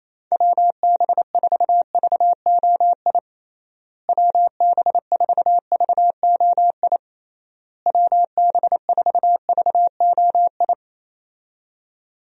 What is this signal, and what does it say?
Morse code, heard as a keyed tone, about 28 wpm, 700 Hz. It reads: WB4VOI WB4VOI WB4VOI
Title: USA Call Signs - 6 Characters - Repeated 3x - 28wpm